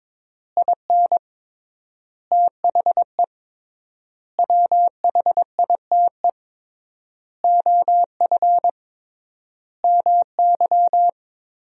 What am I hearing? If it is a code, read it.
Morse code, heard as a keyed tone, about 22 wpm, 700 Hz. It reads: IN THE WHITE OF MY
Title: 3x Word-Spacing - Sentences from Top 500 Words - 22wpm